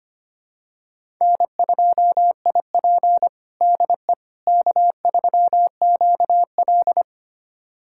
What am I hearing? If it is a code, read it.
Morse code, heard as a keyed tone, about 25 wpm, 700 Hz. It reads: N2IP DE K3QL
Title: USA Callsign DE Callsign - 4 Characters - 25wpm